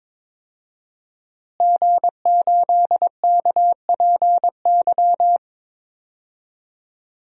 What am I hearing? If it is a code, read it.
Morse code, heard as a keyed tone, about 22 wpm, 700 Hz. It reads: G8KPY
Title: DXCC-100 Call Signs - 5 Characters - Repeated 2x - 22wpm